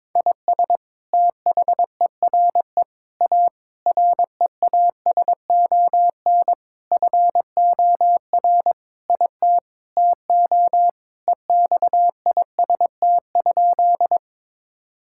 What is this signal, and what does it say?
Morse code, heard as a keyed tone, about 22 wpm, 700 Hz. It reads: IS THERE A REASON FOR IT TO EXIST?
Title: Sentences from Top 1000 Words - Review Part 5 - 22wpm